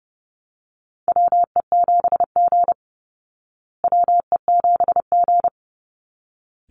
Morse code, heard as a keyed tone, about 30 wpm, 700 Hz. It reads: WE7G WE7G
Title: Call Signs Repeated 3x - 30wpm